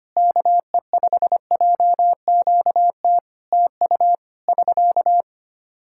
Morse code, heard as a keyed tone, about 25 wpm, 700 Hz. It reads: KE5JQT TU <SK>
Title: Practice QSO - Exchange 4 - Ending - 25wpm